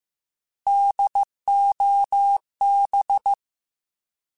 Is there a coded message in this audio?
Yes: Morse code from 0.7 to 3.3 s, heard as a keyed tone, about 15 wpm, 800 Hz. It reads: DOB